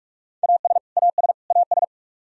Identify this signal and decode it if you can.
Morse code, heard as a keyed tone, about 42 wpm, 700 Hz. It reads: AS AS AS
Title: Top 100 Words Repeated 45wpm